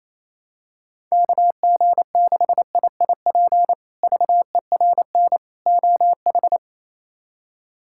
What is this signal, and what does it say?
Morse code, heard as a keyed tone, about 28 wpm, 700 Hz. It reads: KG6IIP VERN OH
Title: K1USN SST QSO Practice - 28wpm